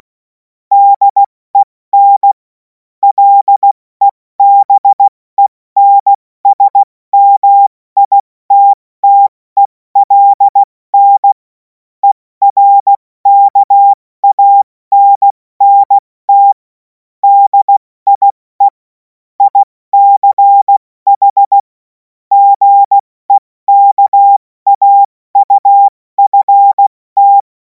Morse code, heard as a keyed tone, about 16 wpm, spaced as for 14 wpm, 800 Hz. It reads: DEN LEBENSMITTELN ERKANNT DIE ICH GEKAUFT
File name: Morsekurs / Witze / Witz 02. Juni 2026 70 F80